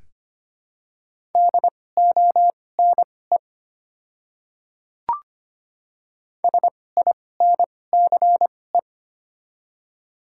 Morse code, heard as a keyed tone, about 25 wpm, spaced as for 18 wpm, 700 Hz. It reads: DONE SINCE
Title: 2x Character Spacing - Top 500 Words - 25wpm